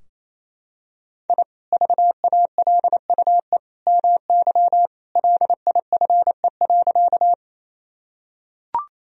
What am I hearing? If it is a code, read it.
Morse code, heard as a keyed tone, about 28 wpm, 700 Hz. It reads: I VALUE MY LIFE.
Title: Sentences from Top 1000 Words - Review Part 3 - 28wpm